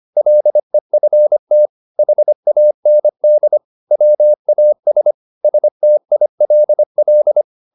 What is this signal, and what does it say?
Morse code, heard as a keyed tone, about 25 wpm, 600 Hz. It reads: LEFT HAND WAS STILL
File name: the old man and the sea part 1 25wpm